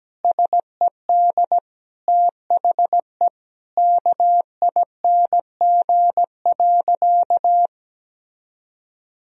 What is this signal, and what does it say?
Morse code, heard as a keyed tone, about 17 wpm, 700 Hz. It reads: SED THE KING.
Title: Sentences from Top 400 Words and Abbreviations 17wpm